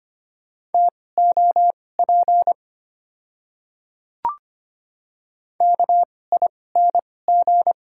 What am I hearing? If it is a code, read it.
Morse code, heard as a keyed tone, about 25 wpm, spaced as for 18 wpm, 700 Hz. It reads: TOP KING
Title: 2x Character Spacing - Top 500 Words - 25wpm